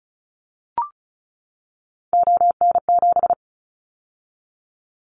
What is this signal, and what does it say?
Morse code, heard as a keyed tone, about 36 wpm, 690 Hz. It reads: ON7
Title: Callsign Prefixes for Top 25 DXCC - 35wpm